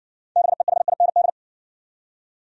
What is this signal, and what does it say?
Morse code, heard as a keyed tone, about 57 wpm, 700 Hz. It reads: BEHIND